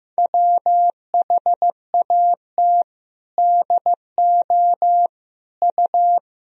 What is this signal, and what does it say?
Morse code, heard as a keyed tone, about 16 wpm, 700 Hz. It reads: WHAT DO U
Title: Sentences from Top 100 Words and Abbreviations 15wpm